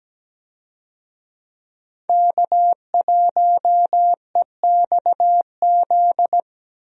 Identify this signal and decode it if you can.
Morse code, heard as a keyed tone, about 17 wpm, 700 Hz. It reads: K1EXZ
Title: USA Call Signs - 5 Characters - Repeated 3x - 17wpm